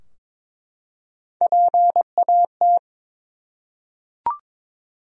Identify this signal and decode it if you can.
Morse code, heard as a keyed tone, about 22 wpm, 700 Hz. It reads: PAT
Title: USA Common Names - Max 3 Letters - 22wpm